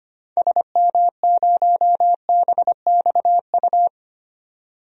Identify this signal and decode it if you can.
Morse code, heard as a keyed tone, about 25 wpm, 700 Hz. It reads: SM0BXU
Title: DXCC-100 Call Signs - 6 Characters - 25wpm